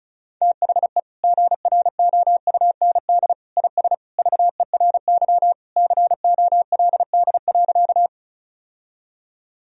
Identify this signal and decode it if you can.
Morse code, heard as a keyed tone, about 36 wpm, 700 Hz. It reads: THE GROUND IS VERY COLD.
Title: Sentences from Top 1000 Words - Review Part 2 - 35wpm